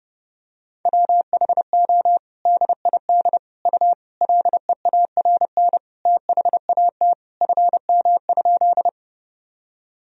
Morse code, heard as a keyed tone, about 32 wpm, 700 Hz. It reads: WHO DID U LEARN THAT FM?
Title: Sentences from Top 300 Words and Abbreviations 30wpm